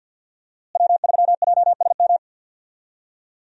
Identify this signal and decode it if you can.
Morse code, heard as a keyed tone, about 47 wpm, 700 Hz. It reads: W3JSM
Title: USA Call Signs - 5 Characters - Repeated 3x - 50wpm